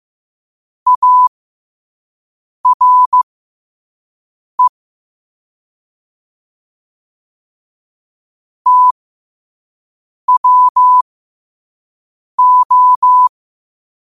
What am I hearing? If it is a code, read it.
Morse code, heard as a keyed tone, about 15 wpm, spaced as for 5 wpm, 1000 Hz. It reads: ARE TWO
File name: qotdfiles / 5wpm